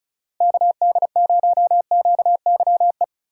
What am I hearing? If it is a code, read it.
Morse code, heard as a keyed tone, about 36 wpm, 700 Hz. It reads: KD0QYE
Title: USA Call Signs - 6 Characters - 2x Character Spacing - 35wpm